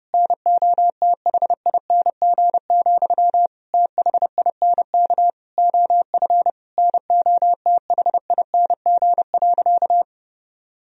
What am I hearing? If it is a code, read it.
Morse code, heard as a keyed tone, about 32 wpm, 700 Hz. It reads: NOTHING, THINK OF NOTHING.
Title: Sentences from Top 1000 Words - Review Part 3 - 30wpm